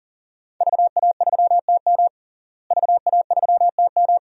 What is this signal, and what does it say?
Morse code, heard as a keyed tone, about 40 wpm, 700 Hz. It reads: VA3TM VA3TM
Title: DXCC-100 Call Signs - 5 Characters - Repeated 2x - 40wpm